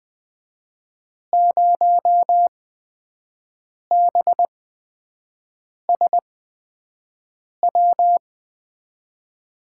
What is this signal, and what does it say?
Morse code, heard as a keyed tone, about 20 wpm, spaced as for 5 wpm, 700 Hz. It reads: 0BSW
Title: USA Call Signs - 6 Characters - 8x Character Spacing - 20wpm